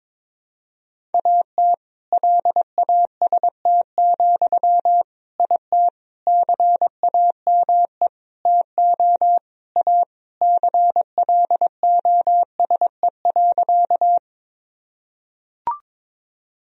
Morse code, heard as a keyed tone, about 22 wpm, 700 Hz. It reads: AT LAST, IT CAME TO A CLOSE.
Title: Sentences from Top 1000 Words - Review Part 1 - 22wpm